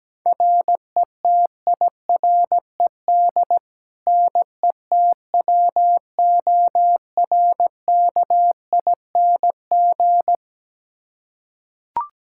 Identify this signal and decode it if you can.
Morse code, heard as a keyed tone, about 17 wpm, 700 Hz. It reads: RETIRED NETWORKING
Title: K1USN SST QSO Practice 17wpm